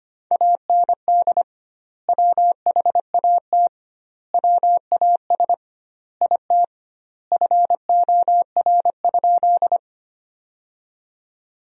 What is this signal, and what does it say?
Morse code, heard as a keyed tone, about 25 wpm, 700 Hz. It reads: AND WHAT WAS IT FOR?
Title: 2x Word-Spacing - Sentences from Top 100 Words - 25wpm